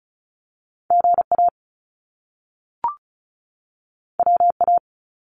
Morse code, heard as a keyed tone, about 35 wpm, 700 Hz. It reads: GA WA
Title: US States 35wpm